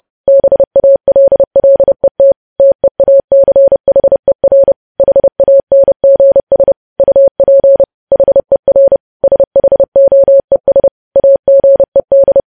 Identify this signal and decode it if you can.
Morse code, heard as a keyed tone, about 30 wpm, 550 Hz. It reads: BALLET TEACHER HANGS UP HER SHOES AGED
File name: podcast / News headlines-1x-30x30